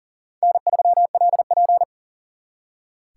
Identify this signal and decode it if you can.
Morse code, heard as a keyed tone, about 51 wpm, 700 Hz. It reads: N3LP